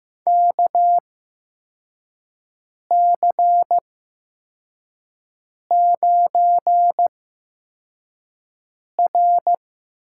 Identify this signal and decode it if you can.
Morse code, heard as a keyed tone, about 15 wpm, spaced as for 4 wpm, 700 Hz. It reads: KC9R
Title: USA Call Signs - 8x Character Spacing - 15wpm